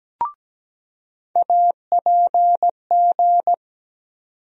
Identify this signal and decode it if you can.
Morse code, heard as a keyed tone, about 17 wpm, 700 Hz. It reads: APG
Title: USA Call Sign Postfixes - 3 Letters 17wpm